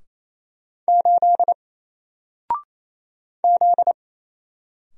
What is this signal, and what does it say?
Morse code, heard as a keyed tone, about 27 wpm, 700 Hz. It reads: 8 Z
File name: Single-Letter-Number-28wpm